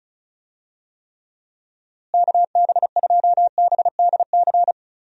Morse code, heard as a keyed tone, about 36 wpm, 700 Hz. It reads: KB2BDC
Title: USA Call Signs - 6 Characters - Repeated 2x - 35wpm